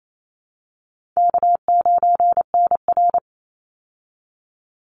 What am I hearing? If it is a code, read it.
Morse code, heard as a keyed tone, about 28 wpm, 700 Hz. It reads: K9NR
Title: Easy Call Signs Repeated 3x - 28wpm